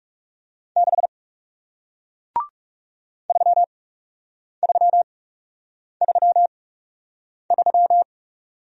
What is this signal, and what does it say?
Morse code, heard as a keyed tone, about 41 wpm, spaced as for 8 wpm, 700 Hz. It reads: 6 3333